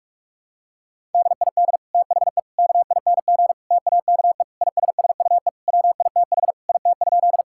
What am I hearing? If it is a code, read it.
Morse code, heard as a keyed tone, about 46 wpm, 700 Hz. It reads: DID THE KING TAKE ISSUE WITH IT?
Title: Sentences from Top 1000 Words - Review Part 6 - 45wpm